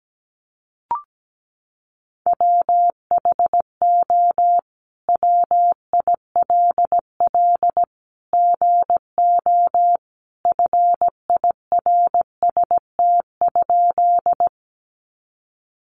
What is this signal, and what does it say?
Morse code, heard as a keyed tone, about 17 wpm, 700 Hz. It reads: WHO WILL GO FIRST?
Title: Sentences from Top 100 Words 17wpm